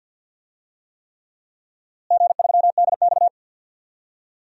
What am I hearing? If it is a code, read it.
Morse code, heard as a keyed tone, about 53 wpm, 700 Hz. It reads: G3DX
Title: DXCC-100 Call Signs - 4 Characters - Repeated 3x - 50wpm